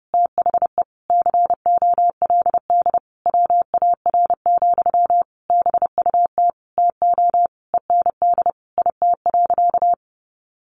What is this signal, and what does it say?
Morse code, heard as a keyed tone, about 30 wpm, 700 Hz. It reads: THE COLD WAR, BUT TO END IT.
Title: Sentences from Top 1200 Words 30wpm